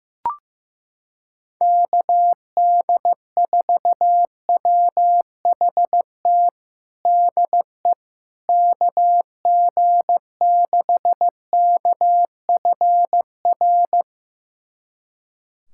Morse code, heard as a keyed tone, about 15 wpm, 700 Hz. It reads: KD4WHT DE KG6KFR
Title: USA Callsign DE Callsign - 6 Characters - 15wpm